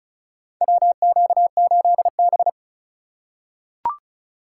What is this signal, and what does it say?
Morse code, heard as a keyed tone, about 34 wpm, 700 Hz. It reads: WQ8B
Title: USA Call Signs - 2x Character Spacing - 35wpm